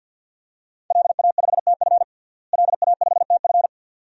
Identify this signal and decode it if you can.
Morse code, heard as a keyed tone, about 52 wpm, 700 Hz. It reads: LA5TF LA5TF
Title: DXCC-100 Call Signs - 5 Characters - Repeated 2x - 50wpm